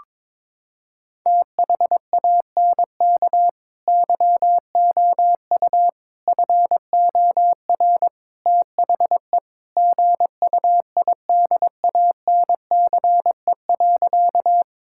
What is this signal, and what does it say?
Morse code, heard as a keyed tone, about 22 wpm, 700 Hz. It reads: THANK YOU FOR THE GUIDANCE.
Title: Sentences from Top 1600 Words 22wpm